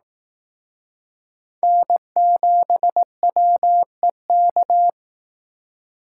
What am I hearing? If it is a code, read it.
Morse code, heard as a keyed tone, about 18 wpm, 700 Hz. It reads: N7WEK
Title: Speed Racing - USA 1x3 Call Signs 15wpm